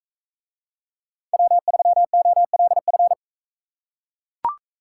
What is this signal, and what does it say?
Morse code, heard as a keyed tone, about 42 wpm, 700 Hz. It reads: W3OLF